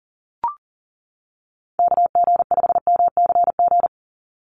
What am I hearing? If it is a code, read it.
Morse code, heard as a keyed tone, about 38 wpm, 700 Hz. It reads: KG5MCG
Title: USA Call Signs 40wpm